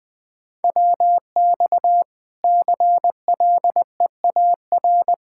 Morse code, heard as a keyed tone, about 21 wpm, 700 Hz. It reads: WX CLEAR
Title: Practice QSO - Exchange 2 - Info - 20wpm